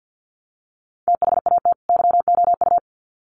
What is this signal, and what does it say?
Morse code, heard as a keyed tone, about 48 wpm, 700 Hz. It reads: THAT YOU
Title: Sets of 2 Words 50wpm